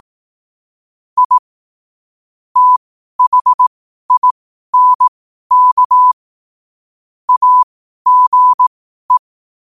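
Morse code, heard as a keyed tone, about 18 wpm, spaced as for 12 wpm, 1000 Hz. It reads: I THINK AGE
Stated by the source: Morse code quotes at 12 words per minute.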